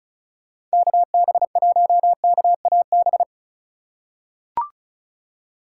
Morse code, heard as a keyed tone, about 36 wpm, 700 Hz. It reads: KB1KAB
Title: USA Call Signs - 6 Characters - 3x Character Spacing - 35wpm